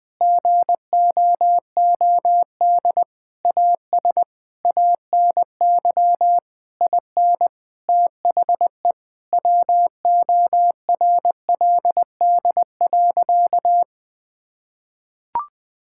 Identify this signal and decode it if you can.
Morse code, heard as a keyed tone, about 20 wpm, 700 Hz. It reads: GOOD AS ANY IN THE WORLD.
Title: Sentences from Top 1600 Words 20wpm